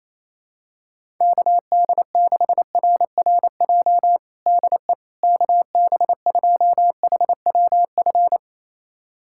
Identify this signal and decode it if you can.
Morse code, heard as a keyed tone, about 28 wpm, 700 Hz. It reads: KD6RRJ DE KB2HWF
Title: USA Callsign DE Callsign - 6 Characters - 28wpm